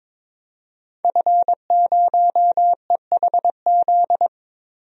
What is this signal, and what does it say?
Morse code, heard as a keyed tone, about 21 wpm, 700 Hz. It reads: F0EHZ
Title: DXCC-100 Call Signs - 5 Characters - 22wpm